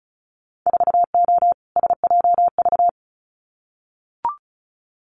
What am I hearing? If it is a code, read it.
Morse code, heard as a keyed tone, about 36 wpm, 700 Hz. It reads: 4O SJV
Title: Winter Field Day Exchanges Encore 35wpm